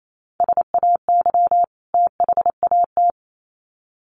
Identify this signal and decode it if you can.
Morse code, heard as a keyed tone, about 28 wpm, 700 Hz. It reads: SAY THAT
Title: Sets of 2 Words 28wpm